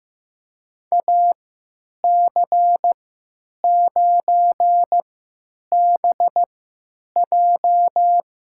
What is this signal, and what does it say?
Morse code, heard as a keyed tone, about 15 wpm, spaced as for 9 wpm, 700 Hz. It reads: AC9BJ
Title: USA Call Signs - 5 Characters - 3x Character Spacing - 15wpm